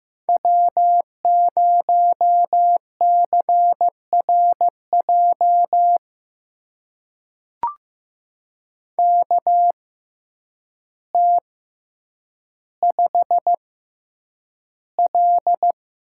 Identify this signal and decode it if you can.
Morse code, heard as a keyed tone, about 15 wpm, spaced as for 8 wpm, 700 Hz. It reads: W0CRJ KT5L
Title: USA Call Signs - 6x Character Spacing - 15wpm